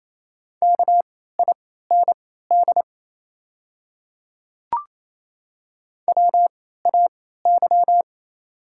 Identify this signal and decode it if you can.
Morse code, heard as a keyed tone, about 27 wpm, spaced as for 16 wpm, 690 Hz. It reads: KIND WAY
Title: 3x Character Spacing - Top 200 Words - 28wpm